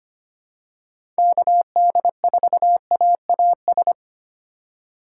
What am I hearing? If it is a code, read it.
Morse code, heard as a keyed tone, about 25 wpm, 700 Hz. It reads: KD4AAS
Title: USA Call Signs - 6 Characters - 4x Character Spacing - 25wpm